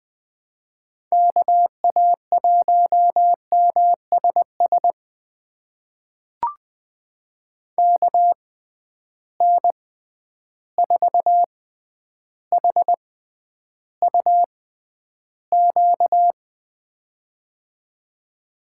Morse code, heard as a keyed tone, about 20 wpm, spaced as for 10 wpm, 700 Hz. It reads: KA1MSS KN4HUQ
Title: USA Call Signs - 6 Characters - 6x Character Spacing - 20wpm